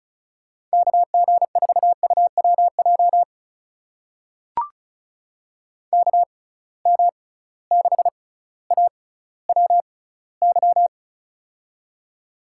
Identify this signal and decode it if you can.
Morse code, heard as a keyed tone, about 34 wpm, spaced as for 18 wpm, 700 Hz. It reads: KG4UWJ KM6AWY
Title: USA Call Signs - 6 Characters - 6x Character Spacing - 35wpm